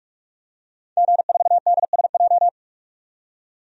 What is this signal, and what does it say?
Morse code, heard as a keyed tone, about 41 wpm, 700 Hz. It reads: G4DSJ
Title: DXCC-100 Call Signs - 5 Characters - 45wpm